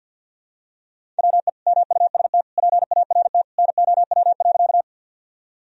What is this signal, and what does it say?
Morse code, heard as a keyed tone, about 46 wpm, 700 Hz. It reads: WE MUST PART NOW.
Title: Sentences from Top 1000 Words - Review Part 1 - 50wpm